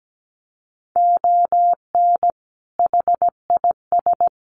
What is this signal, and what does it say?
Morse code, heard as a keyed tone, about 17 wpm, 700 Hz. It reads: ON HIS
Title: Sets of 5 Words Encore2 17wpm